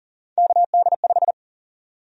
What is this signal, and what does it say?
Morse code, heard as a keyed tone, about 40 wpm, 700 Hz. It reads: KD5
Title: USA Call Sign Prefixes - Repeated 2x - 40wpm